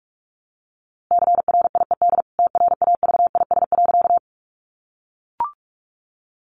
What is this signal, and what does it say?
Morse code, heard as a keyed tone, about 44 wpm, 700 Hz. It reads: CRIED TRAVIS.
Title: A Sound Of Thunder 45wpm